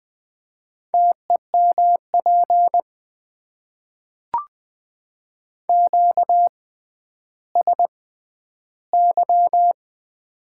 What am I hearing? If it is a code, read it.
Morse code, heard as a keyed tone, about 20 wpm, spaced as for 11 wpm, 700 Hz. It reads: TEMP QSY
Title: Top 90 QSO Elements - 6x Character Spacing - 20wpm